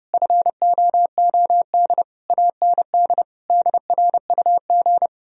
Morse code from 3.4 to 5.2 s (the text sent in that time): DRUG